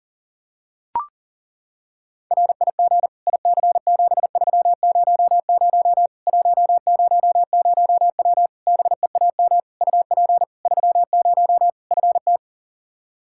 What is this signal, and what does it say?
Morse code, heard as a keyed tone, about 40 wpm, 700 Hz. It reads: RIG IC7300 100W BEAM UP 30 FT
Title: Practice QSO - Exchange 2 - Info - 40wpm